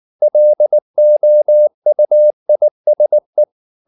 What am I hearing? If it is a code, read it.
Morse code, heard as a keyed tone, about 19 wpm, 600 Hz. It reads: LOUISE
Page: Ferie 19 wpm